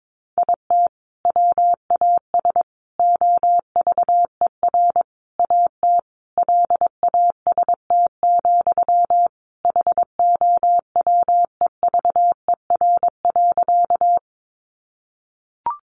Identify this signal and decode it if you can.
Morse code, heard as a keyed tone, about 22 wpm, 700 Hz. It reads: IT WAS OVER AT LAST, HOWEVER.
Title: Sentences from Top 600 Words 22wpm